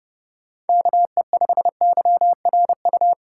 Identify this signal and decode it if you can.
Morse code, heard as a keyed tone, about 30 wpm, 700 Hz. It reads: KE5YRU
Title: USA Call Signs - 6 Characters - 2x Character Spacing - 30wpm